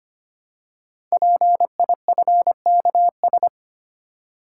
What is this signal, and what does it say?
Morse code, heard as a keyed tone, about 25 wpm, 700 Hz. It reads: PIFKS
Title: Code Groups - 5 Letters - 25wpm